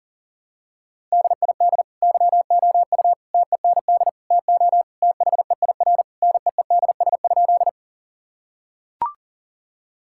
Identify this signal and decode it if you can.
Morse code, heard as a keyed tone, about 37 wpm, 700 Hz. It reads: DID YOU TEND TO THEIR NEEDS?
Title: Sentences from Top 1000 Words - Review Part 5 - 40wpm